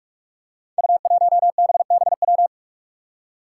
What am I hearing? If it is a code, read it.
Morse code, heard as a keyed tone, about 44 wpm, 700 Hz. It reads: U1BBW
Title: Five Challenging Letter-Number Combinations 45wpm